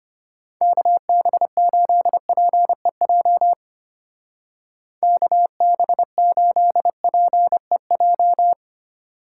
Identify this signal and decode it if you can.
Morse code, heard as a keyed tone, about 32 wpm, 700 Hz. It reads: KB8PEJ KB8PEJ